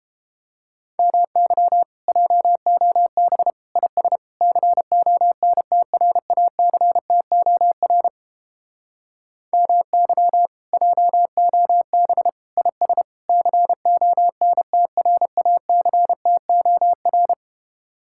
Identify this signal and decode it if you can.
Morse code, heard as a keyed tone, about 29 wpm, 700 Hz. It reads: MY JOB IS CONTRACTOR MY JOB IS CONTRACTOR